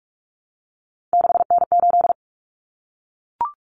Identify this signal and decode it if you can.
Morse code, heard as a keyed tone, about 44 wpm, 700 Hz. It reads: 6N8
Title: Common Cut Numbers - 3-Digit 45wpm